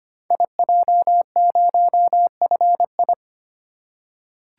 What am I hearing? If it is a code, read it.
Morse code, heard as a keyed tone, about 25 wpm, 700 Hz. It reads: IJ0FI
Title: Code Groups - 5 Letter:Numbers - 25wpm